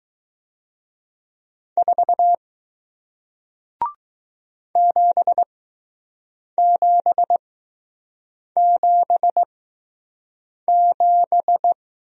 Morse code, heard as a keyed tone, about 19 wpm, spaced as for 6 wpm, 700 Hz. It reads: 4 7777